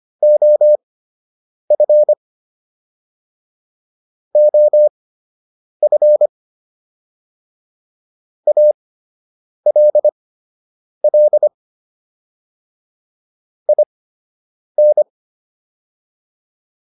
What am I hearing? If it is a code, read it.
Morse code, heard as a keyed tone, about 25 wpm, spaced as for 8 wpm, 600 Hz. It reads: OF OF ALL IN